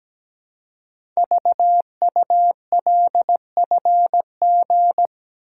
Morse code, heard as a keyed tone, about 17 wpm, 700 Hz. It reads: VULFG
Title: Code Groups - 5 Letters - 17wpm